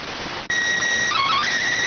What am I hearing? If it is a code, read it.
SSTV (slow-scan television), PD 290.